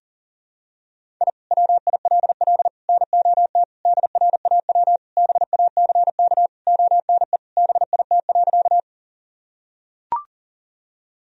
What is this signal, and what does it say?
Morse code, heard as a keyed tone, about 40 wpm, 700 Hz. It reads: I WILL NOT DRAW BACK ONE BIT.
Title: Sentences from Top 1000 Words - Review Part 1 - 40wpm